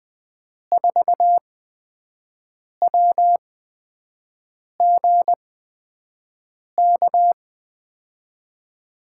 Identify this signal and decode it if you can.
Morse code, heard as a keyed tone, about 21 wpm, spaced as for 6 wpm, 700 Hz. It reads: 4WGK
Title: USA Call Signs - 6 Characters - 8x Character Spacing - 20wpm